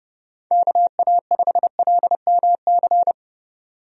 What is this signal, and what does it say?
Morse code, heard as a keyed tone, about 30 wpm, 700 Hz. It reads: KA5LMC
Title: USA Call Signs - 6 Characters - 30wpm